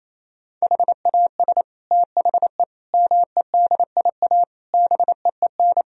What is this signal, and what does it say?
Morse code, heard as a keyed tone, about 28 wpm, 700 Hz. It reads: HAS THE MEDIA BEEN
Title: Sentences from Top 1000 Words - Review Part 4 - 28wpm